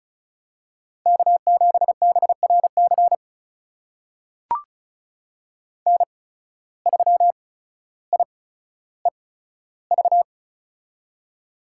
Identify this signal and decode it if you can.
Morse code, heard as a keyed tone, about 36 wpm, spaced as for 15 wpm, 700 Hz. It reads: K7BRC N3IEV
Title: USA Call Signs - 8x Character Spacing - 35wpm